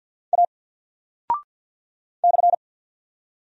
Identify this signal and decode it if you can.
Morse code, heard as a keyed tone, about 48 wpm, 700 Hz. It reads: A /
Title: Single Character (Letter:Number:Punctuation:Prosign) Alt 50wpm